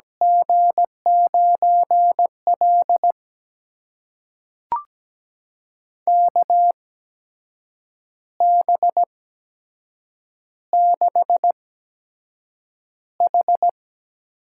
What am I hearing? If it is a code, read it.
Morse code, heard as a keyed tone, about 17 wpm, spaced as for 7 wpm, 700 Hz. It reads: G9L KB6H
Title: USA Call Signs - 8x Character Spacing - 17wpm